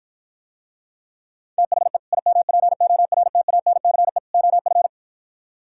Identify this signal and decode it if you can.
Morse code, heard as a keyed tone, about 52 wpm, 700 Hz. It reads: THE IMPORTANCE OF